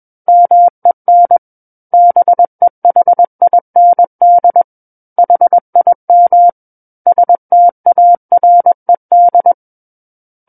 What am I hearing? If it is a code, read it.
Morse code, heard as a keyed tone, about 21 wpm, 700 Hz. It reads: MEN BEHIND HIM STARED
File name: Book One / 12-Chapter 12 - 21 WPM